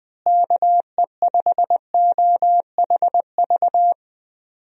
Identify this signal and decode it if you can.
Morse code, heard as a keyed tone, about 21 wpm, 700 Hz. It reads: KE5OHV
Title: USA Call Signs - 6 Characters - Repeated 3x - 20wpm